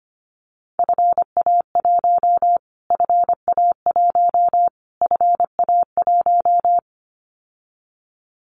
Morse code, heard as a keyed tone, about 25 wpm, 700 Hz. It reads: FA1 FA1 FA1
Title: Callsign Prefixes for Top 25 DXCC - Repeated 3x - 25wpm